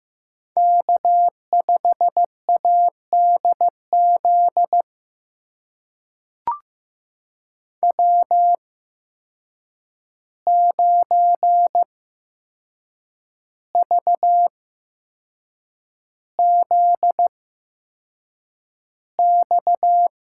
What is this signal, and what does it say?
Morse code, heard as a keyed tone, about 15 wpm, spaced as for 6 wpm, 700 Hz. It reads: K5ADZ W9VZX
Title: USA Call Signs - 5 Characters - 8x Character Spacing - 15wpm